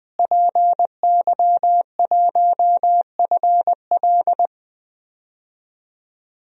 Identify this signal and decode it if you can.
Morse code, heard as a keyed tone, about 20 wpm, 700 Hz. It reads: PY1FL
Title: DXCC-100 Call Signs - 5 Characters - Repeated 2x - 20wpm